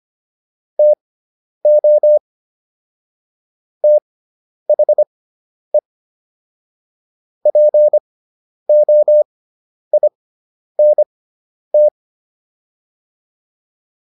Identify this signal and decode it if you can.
Morse code, heard as a keyed tone, about 25 wpm, spaced as for 10 wpm, 600 Hz. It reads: TO THE POINT